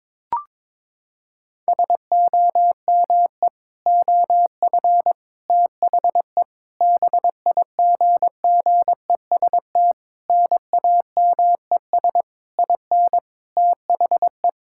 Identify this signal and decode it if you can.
Morse code, heard as a keyed tone, about 22 wpm, 700 Hz. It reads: SOME OF THE BIGGEST NAMES IN THE
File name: practice / 22wpm / Sets of 7 Words - 22wpm